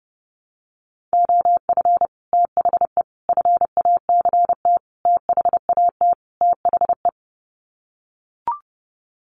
Morse code, heard as a keyed tone, about 30 wpm, 700 Hz. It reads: OF THE FACT THAT THE
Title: Sets of 5 Words Encore2 30wpm